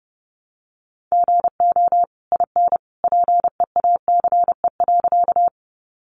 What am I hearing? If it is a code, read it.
Morse code, heard as a keyed tone, about 30 wpm, 700 Hz. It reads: GO IN PEACE.
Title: Sentences from Top 1500 Words 30wpm